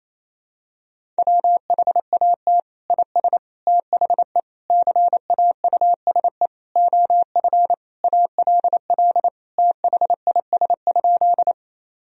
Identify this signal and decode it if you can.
Morse code, heard as a keyed tone, about 28 wpm, 700 Hz. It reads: WHAT IS THE CAUSE OF ALL THIS?
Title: Sentences from Top 1000 Words - Review Part 1 - 28wpm